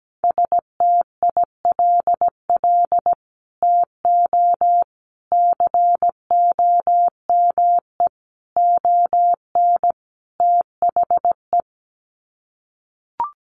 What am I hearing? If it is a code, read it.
Morse code, heard as a keyed tone, about 17 wpm, 700 Hz. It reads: STILL TO COME ON THE
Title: Sets of 5 Words Encore2 17wpm